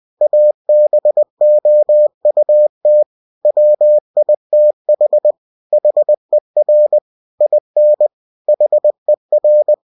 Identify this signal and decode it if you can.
Morse code, heard as a keyed tone, about 20 wpm, 600 Hz. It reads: ABOUT WITH HER IN HER
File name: cw-books / winnie-the-pooh / winnie-20wpm-page-0060